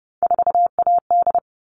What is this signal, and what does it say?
Morse code, heard as a keyed tone, about 30 wpm, 700 Hz. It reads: 4AD
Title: All Cut Numbers - 3-Digit 30wpm